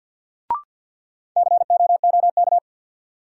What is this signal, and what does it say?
Morse code, heard as a keyed tone, about 46 wpm, 700 Hz. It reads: COOK
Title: Sentences from Top 700 Words - New Word Drill - 50wpm